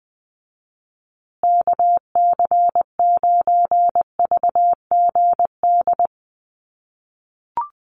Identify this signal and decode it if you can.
Morse code, heard as a keyed tone, about 20 wpm, 700 Hz. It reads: KC9VGD
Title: USA Call Signs Repeated 3x 20wpm